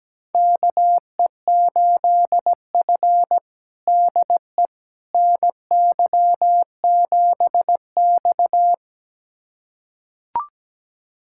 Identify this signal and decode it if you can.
Morse code, heard as a keyed tone, about 17 wpm, 700 Hz. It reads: KE8F DE NY7X
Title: USA Callsign DE Callsign - 4 Characters - 17wpm